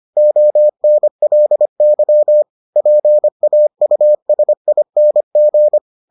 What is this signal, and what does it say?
Morse code, heard as a keyed tone, about 25 wpm, 600 Hz. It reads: ONLY PAUSING